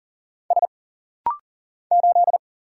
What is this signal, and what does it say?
Morse code, heard as a keyed tone, about 40 wpm, 700 Hz. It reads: S 8
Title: Single Letter-Number Alt - Rapid-Fire 40wpm